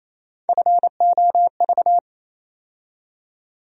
Morse code, heard as a keyed tone, about 27 wpm, 700 Hz. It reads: FOV
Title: USA Call Sign Postfixes - 3 Letters 28wpm